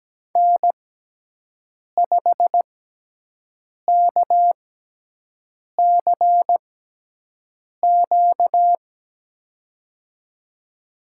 Morse code, heard as a keyed tone, about 17 wpm, spaced as for 6 wpm, 700 Hz. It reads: N5KCQ
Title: USA Call Signs - 5 Characters - 6x Character Spacing - 17wpm